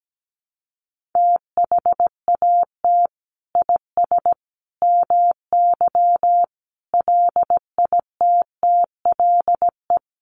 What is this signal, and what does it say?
Morse code, heard as a keyed tone, about 17 wpm, 700 Hz. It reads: THAT IS MY LITTLE
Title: Sentences from Top 1500 Words 17wpm